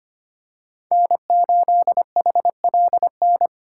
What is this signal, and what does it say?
Morse code, heard as a keyed tone, about 25 wpm, 700 Hz. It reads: N8HLN
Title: USA Call Signs - 5 Characters - 3x Character Spacing - 25wpm